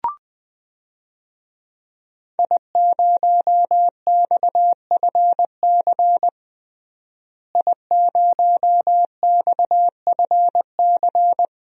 Morse code, heard as a keyed tone, about 20 wpm, 700 Hz. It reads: I0XFC I0XFC
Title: DXCC-100 Call Signs - 5 Characters - Repeated 3x - 20wpm